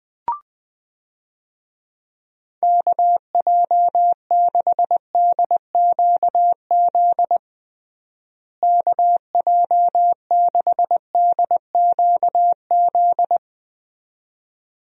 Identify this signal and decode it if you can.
Morse code, heard as a keyed tone, about 20 wpm, 700 Hz. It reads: KJ6DQZ KJ6DQZ
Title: USA Call Signs - 6 Characters - Repeated 2x - 20wpm